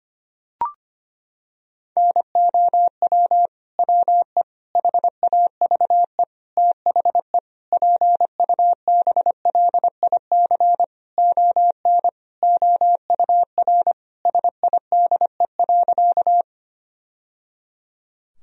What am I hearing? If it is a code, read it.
Morse code, heard as a keyed tone, about 25 wpm, 700 Hz. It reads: NOW WE HAVE THE PUBLIC ON OUR SIDE.
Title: Sentences from Top 1000 Words - Review Part 3 - 25wpm